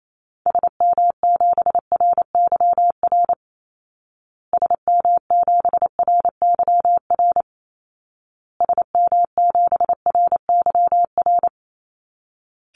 Morse code, heard as a keyed tone, about 28 wpm, 700 Hz. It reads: SM7RYR SM7RYR SM7RYR
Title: Call Signs Repeated 2x - 28wpm